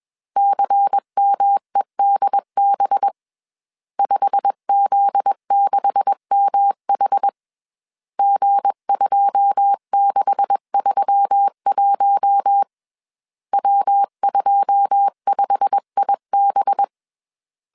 Morse code, heard as a keyed tone, about 21 wpm, 800 Hz. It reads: CMEDB 5Z6MH G2631 W25IB
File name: CW / DET / 1 9 1500C S C 20wpm 2310